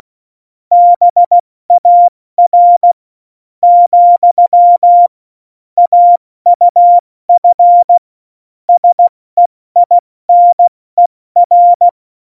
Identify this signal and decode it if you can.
Morse code, heard as a keyed tone, about 16 wpm, spaced as for 13 wpm, 700 Hz. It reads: BAR, AUF SEINER
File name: Morsekurs / Witze / Witz 04. September 2018, 70 F80